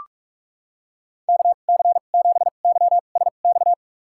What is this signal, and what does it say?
Morse code, heard as a keyed tone, about 46 wpm, spaced as for 33 wpm, 700 Hz. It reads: KC7YSX
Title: USA Call Signs - 2x Character Spacing - 45wpm